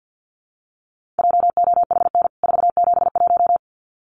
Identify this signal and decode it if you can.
Morse code, heard as a keyed tone, about 50 wpm, 700 Hz. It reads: JOHN 471
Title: CWT Exchange 50wpm